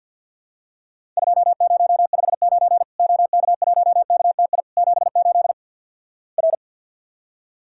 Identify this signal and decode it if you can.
Morse code, heard as a keyed tone, about 49 wpm, 700 Hz. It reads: 2059 OK1KTI 78 R
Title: Worked All Europe DX Contest - QTC Exchange - 50wpm